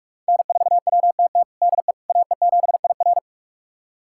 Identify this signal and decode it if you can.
Morse code, heard as a keyed tone, about 46 wpm, 700 Hz. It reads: N4WTT DE AE7IR
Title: USA Callsign DE Callsign - 5 Characters - 45wpm